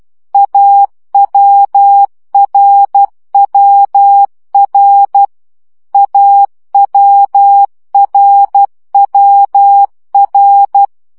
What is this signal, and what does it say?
Morse code, heard as a keyed tone, about 12 wpm, 800 Hz. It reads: AWRWR AWRWR
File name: CW gyakorlas a r w 10 baud